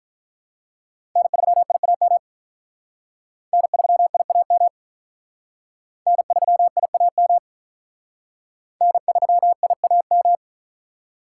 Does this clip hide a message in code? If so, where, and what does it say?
Yes: Morse code from 1.2 to 10.4 s, heard as a keyed tone, about 44 wpm, 700 Hz. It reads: N3IAM N3IAM N3IAM N3IAM